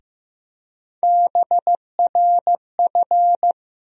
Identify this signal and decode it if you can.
Morse code, heard as a keyed tone, about 15 wpm, 700 Hz. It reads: BRF
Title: USA Call Sign Postfixes - 3 Letters 15wpm